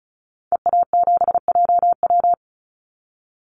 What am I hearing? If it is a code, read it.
Morse code, heard as a keyed tone, about 34 wpm, 700 Hz. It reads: EA7JW
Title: Call Signs Repeated 2x - 35wpm